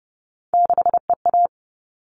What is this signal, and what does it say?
Morse code, heard as a keyed tone, about 30 wpm, 700 Hz. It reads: 6EA
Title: All Cut Numbers - 3-Digit 30wpm